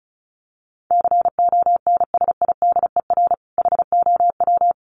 Morse code, heard as a keyed tone, about 34 wpm, 700 Hz. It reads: CONSIDER HOW
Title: Sentences from Top 1000 Words 35wpm